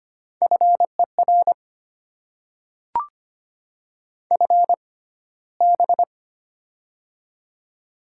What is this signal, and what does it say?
Morse code, heard as a keyed tone, about 25 wpm, spaced as for 15 wpm, 690 Hz. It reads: FER FB
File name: Top 20 QSO Elements - 6x Character Spacing - 25wpm